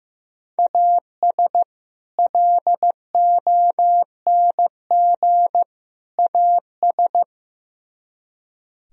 Morse code, heard as a keyed tone, about 16 wpm, 700 Hz. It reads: AS LONG AS
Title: Speed-Racing - Sets of 3 Words 15wpm